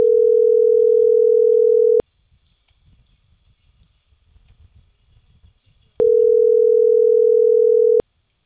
Ringback tone (North America), 440 + 480 Hz.